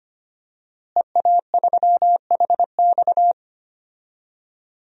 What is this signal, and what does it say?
Morse code, heard as a keyed tone, about 25 wpm, 700 Hz. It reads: EA3HX
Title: DXCC-100 Call Signs - 5 Characters - Repeated 3x - 25wpm